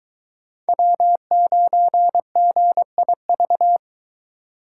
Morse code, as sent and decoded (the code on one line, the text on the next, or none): .-- ----. --. .. ...-
W9GIV